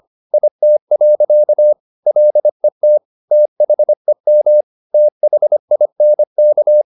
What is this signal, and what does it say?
Morse code, heard as a keyed tone, about 25 wpm, 600 Hz. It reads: IT. LET THEM THINK